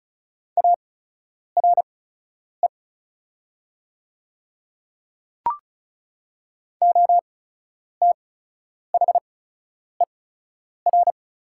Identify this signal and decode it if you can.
Morse code, heard as a keyed tone, about 34 wpm, spaced as for 10 wpm, 700 Hz. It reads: ARE OTHER
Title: 8x Character Spacing - Top 100 Words - 35wpm